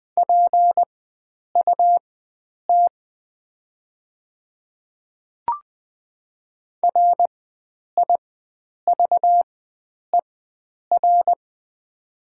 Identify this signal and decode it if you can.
Morse code, heard as a keyed tone, about 20 wpm, spaced as for 9 wpm, 700 Hz. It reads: PUT RIVER
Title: 4x Character Spacing - Top 300 Words - 20wpm